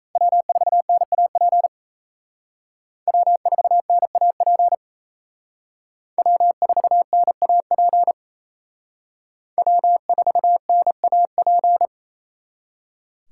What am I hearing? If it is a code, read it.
Morse code, heard as a keyed tone, about 36 wpm, 700 Hz. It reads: W4NAP W4NAP W4NAP W4NAP